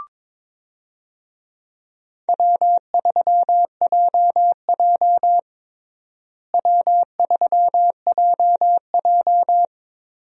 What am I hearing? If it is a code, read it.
Morse code, heard as a keyed tone, about 22 wpm, 700 Hz. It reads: W3JJ W3JJ
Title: USA Call Signs - 4 Characters - Repeated 3x - 22wpm